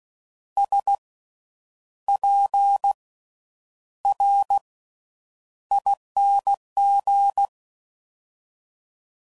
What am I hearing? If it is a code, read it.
Morse code, heard as a keyed tone, about 16 wpm, spaced as for 8 wpm, 800 Hz. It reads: SPRING
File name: Suffixes / Special INGs / Slow With Faster Ings